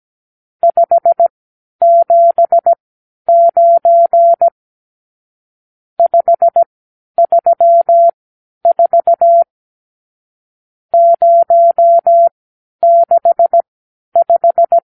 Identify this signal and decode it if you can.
Morse code, heard as a keyed tone, about 17 wpm, spaced as for 10 wpm, 680 Hz. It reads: 579 534 065
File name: arrl / Review (1-0)